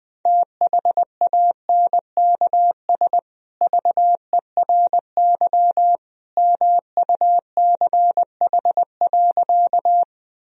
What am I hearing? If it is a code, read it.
Morse code, heard as a keyed tone, about 20 wpm, 700 Hz. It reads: THANKS VERY MUCH.
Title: Sentences from Top 1000 Words - Review Part 3 - 20wpm